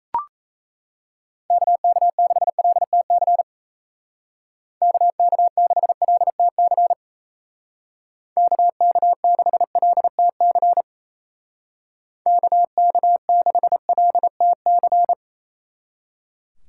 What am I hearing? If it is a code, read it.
Morse code, heard as a keyed tone, about 37 wpm, 700 Hz. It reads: KK6LTC KK6LTC KK6LTC KK6LTC